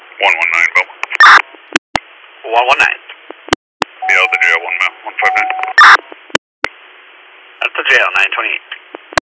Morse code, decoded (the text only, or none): none